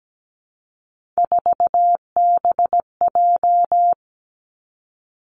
Morse code, heard as a keyed tone, about 17 wpm, 700 Hz. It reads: 4BJ
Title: Three Challenging Letter-Number Combinations 17wpm